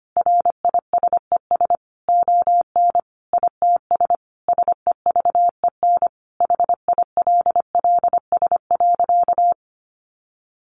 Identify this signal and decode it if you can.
Morse code, heard as a keyed tone, about 25 wpm, 700 Hz. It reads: RISES ON ITS SEVEN HILLS.
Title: A Sound Of Thunder 25wpm